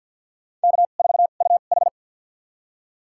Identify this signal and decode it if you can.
Morse code, heard as a keyed tone, about 48 wpm, spaced as for 35 wpm, 700 Hz. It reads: K4UH
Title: USA Call Signs - 2x Character Spacing - 50wpm